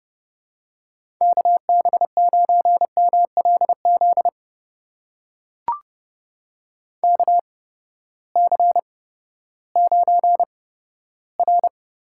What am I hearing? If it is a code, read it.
Morse code, heard as a keyed tone, about 30 wpm, spaced as for 15 wpm, 700 Hz. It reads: KB9MLZ KC9R
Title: USA Call Signs - 8x Character Spacing - 30wpm